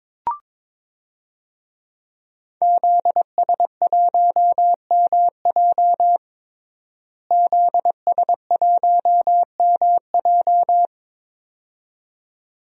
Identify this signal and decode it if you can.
Morse code, heard as a keyed tone, about 22 wpm, 700 Hz. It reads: ZS1MJ ZS1MJ
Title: DXCC-100 Call Signs - 5 Characters - Repeated 2x - 22wpm